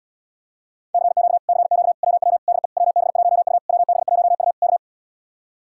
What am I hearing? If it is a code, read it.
Morse code, heard as a keyed tone, about 80 wpm, 700 Hz. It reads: CQ CQ WW DE WB1U WB1U K